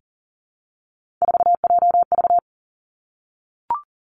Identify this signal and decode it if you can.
Morse code, heard as a keyed tone, about 40 wpm, 700 Hz. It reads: 4JV
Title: Three Challenging Letter-Number Combinations 40wpm